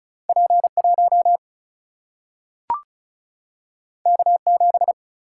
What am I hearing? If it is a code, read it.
Morse code, heard as a keyed tone, about 36 wpm, 700 Hz. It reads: P1 K7
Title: Single-Letter Single-Number 35wpm